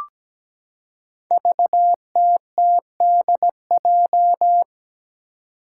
Morse code, heard as a keyed tone, about 17 wpm, 700 Hz. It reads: VTTDJ
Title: Five Challenging Letter-Number Combinations 17wpm